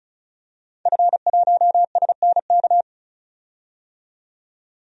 Morse code, heard as a keyed tone, about 36 wpm, 700 Hz. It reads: F1SNK
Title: DXCC-100 Call Signs - 5 Characters - 35wpm